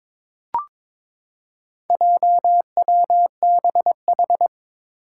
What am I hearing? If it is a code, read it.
Morse code, heard as a keyed tone, about 22 wpm, 700 Hz. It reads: JWBH
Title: Four Challenging Letter-Number Combinations 22wpm